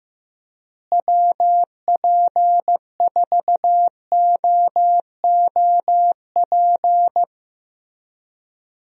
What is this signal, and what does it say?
Morse code, heard as a keyed tone, about 15 wpm, 700 Hz. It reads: WP4OOP
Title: USA Call Signs - 6 Characters - Repeated 3x - 15wpm